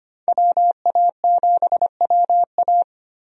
Morse code, heard as a keyed tone, about 25 wpm, 700 Hz. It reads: WA7WA
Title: USA Call Signs - 5 Characters - 25wpm